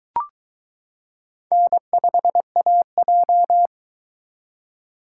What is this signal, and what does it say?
Morse code, heard as a keyed tone, about 22 wpm, 700 Hz. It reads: N5AJ